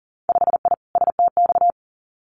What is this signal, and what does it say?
Morse code, heard as a keyed tone, about 42 wpm, 700 Hz. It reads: 5I STX
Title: Winter Field Day Exchanges Encore 40wpm